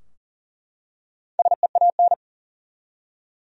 Morse code, heard as a keyed tone, about 40 wpm, 700 Hz. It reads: SEAN
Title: USA Common Names - Max 5 Letters - 40wpm